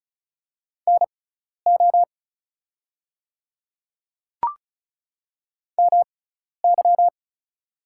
Morse code, heard as a keyed tone, about 36 wpm, spaced as for 12 wpm, 700 Hz. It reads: NO MY
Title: 6x Character Spacing - Top 200 Words - 35wpm